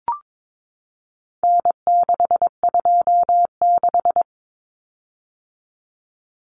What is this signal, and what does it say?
Morse code, heard as a keyed tone, about 22 wpm, 700 Hz. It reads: N626
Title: Common Cut Numbers - 4-Digit 22wpm